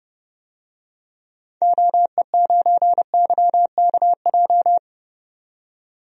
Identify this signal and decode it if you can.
Morse code, heard as a keyed tone, about 32 wpm, 700 Hz. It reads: OE9YKJ
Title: DXCC-100 Call Signs - 6 Characters - 30wpm